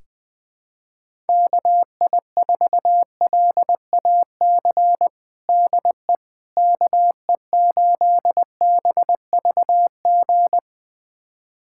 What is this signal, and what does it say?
Morse code, heard as a keyed tone, about 21 wpm, 700 Hz. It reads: KI4LAC DE KE8BVG
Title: USA Callsign DE Callsign - 6 Characters - 20wpm